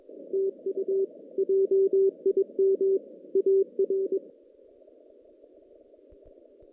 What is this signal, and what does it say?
Morse code, heard as a keyed tone, about 22 wpm, 390 Hz. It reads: TU JIM AR
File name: 2016 JAN CW NAQP / 2016 Jan naqp k5ot - 0-9